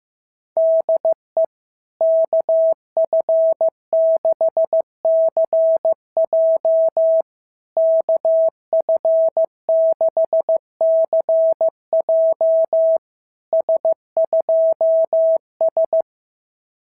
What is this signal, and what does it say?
Morse code, heard as a keyed tone, about 15 wpm, 660 Hz. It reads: DE KF6CJ KF6CJ S2S
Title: SOTA QSO Practice 15wpm